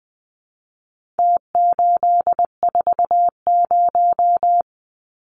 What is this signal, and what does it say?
Morse code, heard as a keyed tone, about 20 wpm, 700 Hz. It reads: T840
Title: Numbers - Contest Style - 4-Digit 20wpm